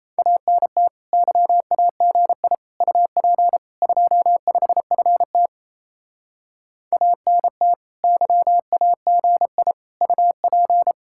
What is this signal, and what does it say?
Morse code, heard as a keyed tone, about 34 wpm, 700 Hz. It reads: ANT YAGI UP 25FT ANT YAGI UP
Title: Speed-Racing Common QSO Phrases - 28wpm